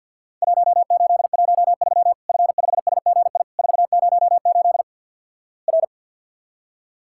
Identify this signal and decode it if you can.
Morse code, heard as a keyed tone, about 50 wpm, 700 Hz. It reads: 1813 F5SGI 408 R
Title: Worked All Europe DX Contest - QTC Exchange - 50wpm